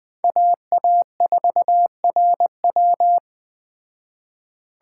Morse code, heard as a keyed tone, about 20 wpm, 700 Hz. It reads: AA4RW